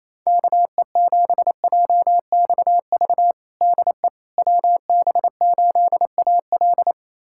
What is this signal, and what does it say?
Morse code, heard as a keyed tone, about 28 wpm, 700 Hz. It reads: KE7JXV DE WB8AL
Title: USA Callsign DE Callsign 28wpm